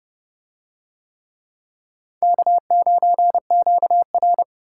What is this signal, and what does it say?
Morse code, heard as a keyed tone, about 26 wpm, 700 Hz. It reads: K9QR